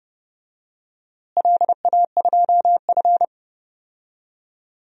Morse code, heard as a keyed tone, about 30 wpm, 700 Hz. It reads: LA2F
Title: DXCC-100 Call Signs - 4 Characters - Repeated 2x - 30wpm